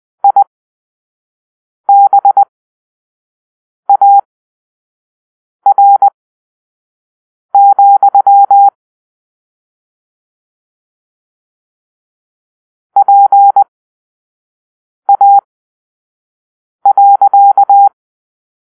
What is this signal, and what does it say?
Morse code, heard as a keyed tone, about 20 wpm, spaced as for 5 wpm, 800 Hz. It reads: IBAR, PA.
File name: arrl / Grp-7 Drill-3